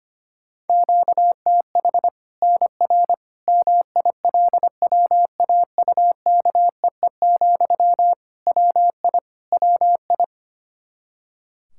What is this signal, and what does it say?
Morse code, heard as a keyed tone, about 25 wpm, 700 Hz. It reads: QTH NR MILWAUKEE, WI WI
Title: K1USN SST QSO Practice 25wpm